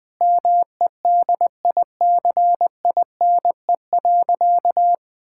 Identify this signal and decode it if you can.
Morse code, heard as a keyed tone, about 21 wpm, 700 Hz. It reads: MEDICINE.
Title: Sentences from Top 1000 Words - Review Part 6 - 20wpm